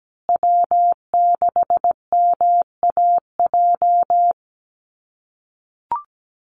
Morse code, heard as a keyed tone, about 17 wpm, 700 Hz. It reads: W6MAJ
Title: USA Call Signs Repeated 3x 17wpm